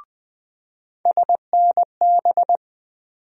Morse code, heard as a keyed tone, about 21 wpm, 700 Hz. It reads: SNB
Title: USA Call Sign Postfixes - 3 Letters 20wpm